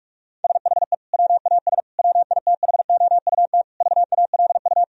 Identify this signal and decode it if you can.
Morse code, heard as a keyed tone, about 46 wpm, 700 Hz. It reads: SHE WAS WITHOUT VALU
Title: Sentences from Top 1000 Words - Review Part 3 - 45wpm